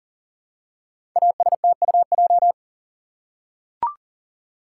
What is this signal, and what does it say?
Morse code, heard as a keyed tone, about 40 wpm, 700 Hz. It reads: ASTUJ
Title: Code Groups - 5 Letters - 40wpm